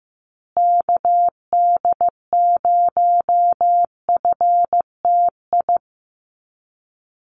Morse code, heard as a keyed tone, about 14 wpm, 700 Hz. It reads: KD0FTI
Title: USA Call Signs Repeated 3x 15wpm